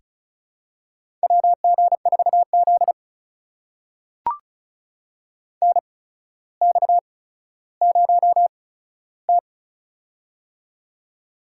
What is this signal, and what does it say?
Morse code, heard as a keyed tone, about 36 wpm, spaced as for 15 wpm, 700 Hz. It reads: WG4Z NX0T
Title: USA Call Signs - 4 Characters - 8x Character Spacing - 35wpm